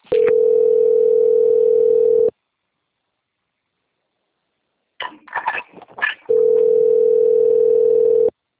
Ringback tone (North America), 440 + 480 Hz.